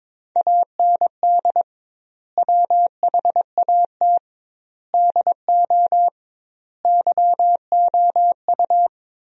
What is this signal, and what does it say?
Morse code, heard as a keyed tone, about 22 wpm, 700 Hz. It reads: AND WHAT DO YOU
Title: 2x Word-Spacing - Sentences from Top 100 Words - 22wpm